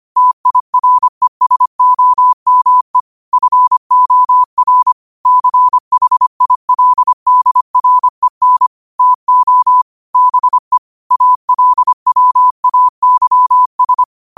Morse code, heard as a keyed tone, about 25 wpm, 1000 Hz. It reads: TIRESOME FOR CHILDREN TO BE ALWAYS
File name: qotdfiles / 25wpm